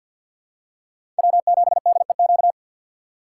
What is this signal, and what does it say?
Morse code, heard as a keyed tone, about 51 wpm, 700 Hz. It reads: W7DEQ
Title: USA Call Signs - 5 Characters - 4x Character Spacing - 50wpm